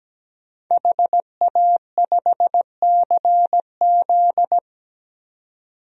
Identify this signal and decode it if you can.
Morse code, heard as a keyed tone, about 17 wpm, 700 Hz. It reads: HA5CZ
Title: DXCC-100 Call Signs - 5 Characters - 17wpm